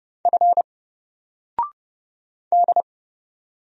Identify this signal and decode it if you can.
Morse code, heard as a keyed tone, about 31 wpm, 700 Hz. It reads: F D
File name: Single-Letter-Number-30wpm